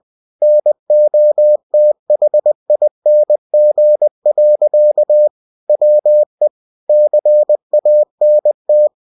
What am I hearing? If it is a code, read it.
Morse code, heard as a keyed tone, about 20 wpm, 600 Hz. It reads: NOTHING. WE CANT
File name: cw-books / winnie-the-pooh / winnie-20wpm-page-0045